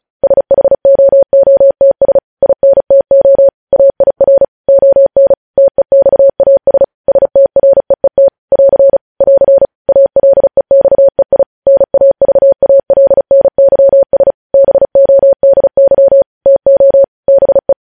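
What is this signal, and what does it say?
Morse code, heard as a keyed tone, about 36 wpm, 550 Hz. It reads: SHOOTS INTO AIR ON TEXAS STREET <AR> <AR> ALEXEI NAVALNYS BODY TO BE
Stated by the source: Morse news headlines sent at 35 words per minute.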